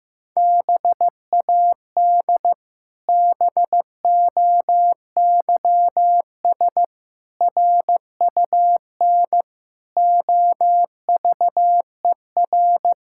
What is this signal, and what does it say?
Morse code, heard as a keyed tone, about 15 wpm, 700 Hz. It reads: BAD BOYS RUN OVER
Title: Pangram Challenge v2 - 15wpm